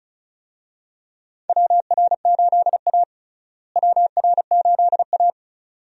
Morse code, heard as a keyed tone, about 36 wpm, 700 Hz. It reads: WR8A WR8A
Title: USA Call Signs - 4 Characters - Repeated 2x - 35wpm